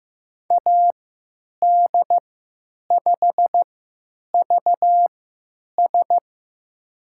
Morse code, heard as a keyed tone, about 15 wpm, spaced as for 9 wpm, 700 Hz. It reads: AD5VS
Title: USA Call Signs - 5 Characters - 3x Character Spacing - 15wpm